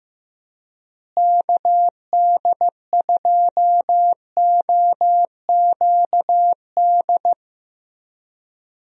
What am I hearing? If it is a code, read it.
Morse code, heard as a keyed tone, about 15 wpm, 700 Hz. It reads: KD2OQD
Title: USA Call Signs - 6 Characters - Repeated 3x - 15wpm